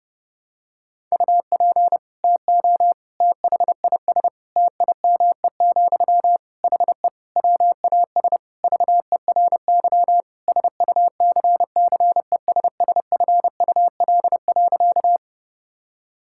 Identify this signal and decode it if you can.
Morse code, heard as a keyed tone, about 30 wpm, 700 Hz. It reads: UP TO THIS TIME, HE WAS VERY SUCCESSFUL.
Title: Sentences from Top 1000 Words - Review Part 6 - 30wpm